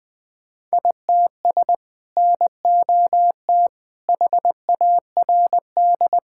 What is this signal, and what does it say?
Morse code, heard as a keyed tone, about 20 wpm, 700 Hz. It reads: ITS NOT HARD
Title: Sets of 6 Words - 20wpm